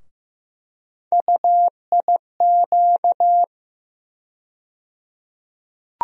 Morse code, heard as a keyed tone, about 15 wpm, 700 Hz. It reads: UIQ
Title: USA Call Sign Postfixes - 3 Letters 15wpm